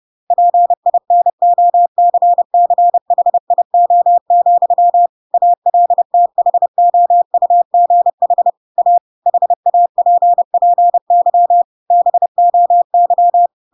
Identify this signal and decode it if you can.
Morse code, heard as a keyed tone, about 30 wpm, 700 Hz. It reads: PINOCCHIO, ALTHOUGH A HAPPY BOY
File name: Pinocchio / Pinocchio ch8 30 30 ik1hge